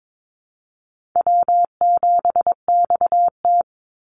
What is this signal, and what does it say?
Morse code, heard as a keyed tone, about 22 wpm, 700 Hz. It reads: W7XT
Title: Easy Call Signs Repeated 3x - 22wpm